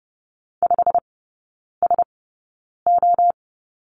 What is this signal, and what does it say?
Morse code, heard as a keyed tone, about 30 wpm, spaced as for 9 wpm, 700 Hz. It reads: 5SO
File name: Lesson 19e - Letter w - Three Characters 30wpm